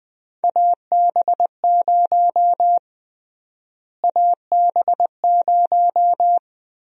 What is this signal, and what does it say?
Morse code, heard as a keyed tone, about 20 wpm, 700 Hz. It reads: AB0 AB0
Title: USA Call Sign Prefixes - Repeated 2x - 20wpm